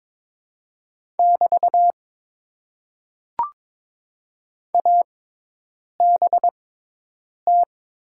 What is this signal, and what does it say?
Morse code, heard as a keyed tone, about 22 wpm, spaced as for 8 wpm, 700 Hz. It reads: <BT> ABT
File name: Top 90 QSO Elements - 6x Character Spacing - 22wpm